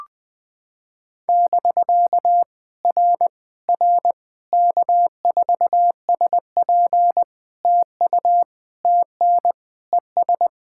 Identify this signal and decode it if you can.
Morse code, heard as a keyed tone, about 20 wpm, 700 Hz. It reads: <BK> R R K4SP TU TN ES
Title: POTA QSO Practice - 20wpm